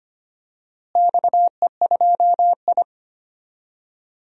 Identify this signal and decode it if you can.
Morse code, heard as a keyed tone, about 25 wpm, 700 Hz. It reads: XE2I
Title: DXCC-100 Call Signs - 4 Characters - Repeated 2x - 25wpm